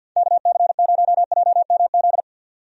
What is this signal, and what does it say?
Morse code, heard as a keyed tone, about 51 wpm, 700 Hz. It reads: KC0JMZ
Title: USA Call Signs - 6 Characters - 50wpm